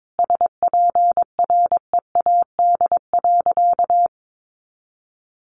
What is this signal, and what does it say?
Morse code, heard as a keyed tone, about 22 wpm, 700 Hz. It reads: SPREAD.
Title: Sentences from Top 1200 Words 22wpm